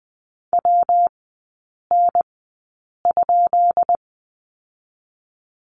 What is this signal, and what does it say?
Morse code, heard as a keyed tone, about 19 wpm, spaced as for 8 wpm, 700 Hz. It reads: WN?
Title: 133 Lesson 22e - Question Mark - Three Characters 20wpm